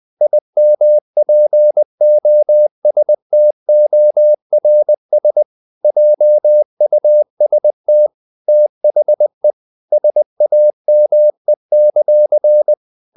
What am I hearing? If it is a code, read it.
Morse code, heard as a keyed tone, about 20 wpm, 600 Hz. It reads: IMPOSTORS JUST THE SAME;
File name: CW / Practice / If20